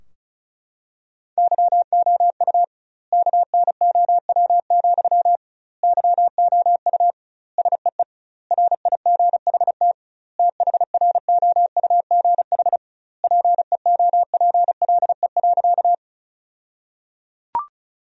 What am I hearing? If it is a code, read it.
Morse code, heard as a keyed tone, about 36 wpm, 700 Hz. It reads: YOU KNOW, YOU SEE RIGHT THROUGH PEOPLE.
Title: 2x Word-Spacing - Sentences from Top 200 Words - 35wpm